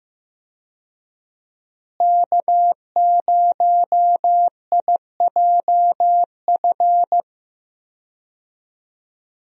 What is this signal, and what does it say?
Morse code, heard as a keyed tone, about 15 wpm, 700 Hz. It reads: K0IJF
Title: USA Call Signs - 5 Characters - Repeated 3x - 15wpm